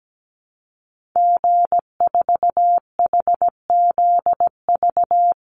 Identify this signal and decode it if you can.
Morse code, heard as a keyed tone, about 17 wpm, 700 Hz. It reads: G4HZV
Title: Call Signs Repeated 3x - 17wpm